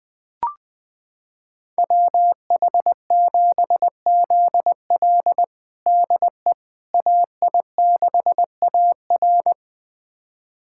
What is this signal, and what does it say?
Morse code, heard as a keyed tone, about 20 wpm, 700 Hz. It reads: WH7ZL DE AI6AR
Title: USA Callsign DE Callsign - 5 Characters - 20wpm